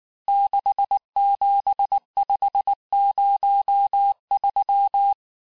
Morse code, heard as a keyed tone, about 19 wpm, 800 Hz. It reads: 67503
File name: cours-du-lundi files / cours-1595-teste-CHIFFRE-vitesse18-MPM